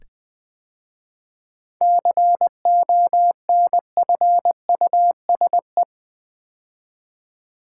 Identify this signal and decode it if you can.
Morse code, heard as a keyed tone, about 20 wpm, 700 Hz. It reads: CONFUSE
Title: 7 Letter Words - First 3 Letters Given - 20wpm